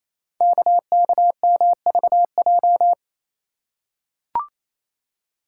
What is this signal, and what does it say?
Morse code, heard as a keyed tone, about 27 wpm, 700 Hz. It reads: KKMVJ
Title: Code Groups - 5 Letters - 28wpm